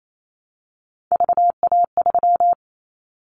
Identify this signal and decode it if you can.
Morse code, heard as a keyed tone, about 28 wpm, 700 Hz. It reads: VA3
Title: Callsign Prefixes for Top 25 DXCC - Repeated 3x - 28wpm